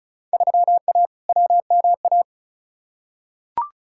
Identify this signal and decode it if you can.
Morse code, heard as a keyed tone, about 36 wpm, 700 Hz. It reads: 3A WMA
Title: Field Day Exchanges Encore - 35wpm